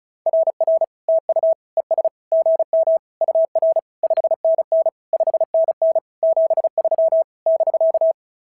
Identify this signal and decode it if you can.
Morse code, heard as a keyed tone, about 33 wpm, 650 Hz. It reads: RR TU ES GM UR 5NN 5NN 73 <BK>
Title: SOTA QSO Practice 35wpm